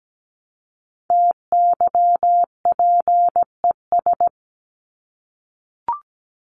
Morse code, heard as a keyed tone, about 17 wpm, 700 Hz. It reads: TYPES
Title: ES:S Suffix - Top 100 Words with Base Words 17wpm